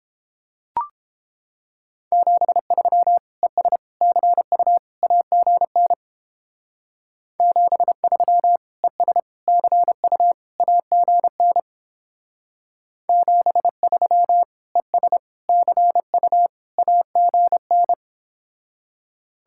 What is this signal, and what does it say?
Morse code, heard as a keyed tone, about 29 wpm, 700 Hz. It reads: 73 ES CU AGN 73 ES CU AGN 73 ES CU AGN